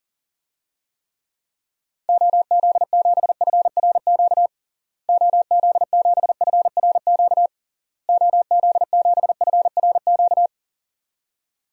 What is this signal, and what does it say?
Morse code, heard as a keyed tone, about 40 wpm, 700 Hz. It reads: OZ7FRQ OZ7FRQ OZ7FRQ
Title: DXCC-100 Call Signs - 6 Characters - Repeated 3x - 40wpm